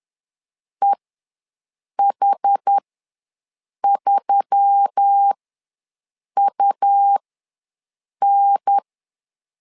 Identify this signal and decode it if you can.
Morse code, heard as a keyed tone, about 10 wpm, spaced as for 6 wpm, 800 Hz. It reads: EH3UN
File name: LEZIONI / 4 Q1S3 ws 10wpm 2437